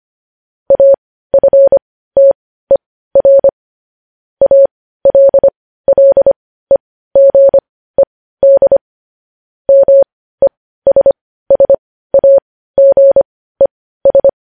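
Morse code, heard as a keyed tone, about 25 wpm, spaced as for 15 wpm, 550 Hz. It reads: AFTER ALLEGED MESSAGES
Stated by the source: Morse news headlines practice at 15 wpm.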